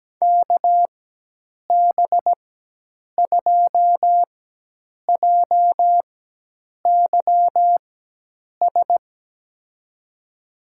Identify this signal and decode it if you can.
Morse code, heard as a keyed tone, about 17 wpm, spaced as for 8 wpm, 700 Hz. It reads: KB2JYS
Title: USA Call Signs - 6 Characters - 4x Character Spacing - 17wpm